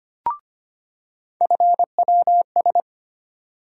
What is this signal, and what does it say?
Morse code, heard as a keyed tone, about 25 wpm, 700 Hz. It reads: FWS
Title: USA Call Sign Postfixes - 3 Letters 25wpm